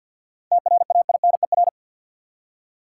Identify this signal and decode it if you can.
Morse code, heard as a keyed tone, about 52 wpm, 700 Hz. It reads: TRAINER
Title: Seven Letter Words - 50wpm